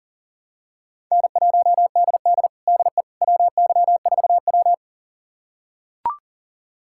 Morse code, heard as a keyed tone, about 40 wpm, 700 Hz. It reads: N1DD DE WY4W
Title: USA Callsign DE Callsign - 4 Characters - 40wpm